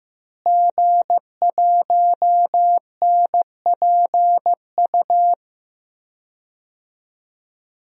Morse code, heard as a keyed tone, about 15 wpm, 700 Hz. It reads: G1NPU
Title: DXCC-100 Call Signs - 5 Characters - Repeated 2x - 15wpm